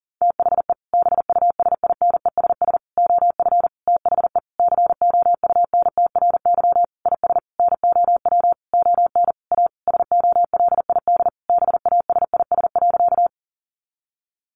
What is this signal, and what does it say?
Morse code, heard as a keyed tone, about 37 wpm, 700 Hz. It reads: THE BUSINESS OF THE COUNTRY IS NOW ON A SOLID BASIS.
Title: Sentences from Top 1500 Words 40wpm